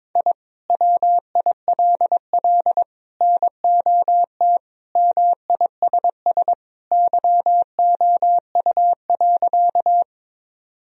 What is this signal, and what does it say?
Morse code, heard as a keyed tone, about 22 wpm, 700 Hz. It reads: I WILL NOT MISS YOU.
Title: Sentences from Top 1000 Words - Review Part 3 - 22wpm